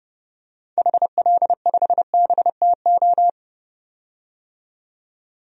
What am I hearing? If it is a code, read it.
Morse code, heard as a keyed tone, about 31 wpm, 700 Hz. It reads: HL5BTO
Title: DXCC-100 Call Signs - 6 Characters - 30wpm